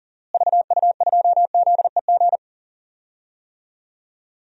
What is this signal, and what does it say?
Morse code, heard as a keyed tone, about 37 wpm, 700 Hz. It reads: VU2ZEG
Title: DXCC-100 Call Signs - 6 Characters - Repeated 3x - 40wpm